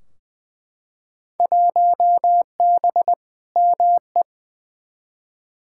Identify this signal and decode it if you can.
Morse code, heard as a keyed tone, about 21 wpm, 700 Hz. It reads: 1B ME
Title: Field Day Exchanges - 20wpm